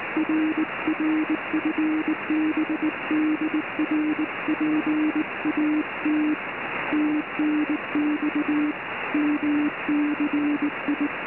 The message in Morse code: RR FB DR PAT TNX MCI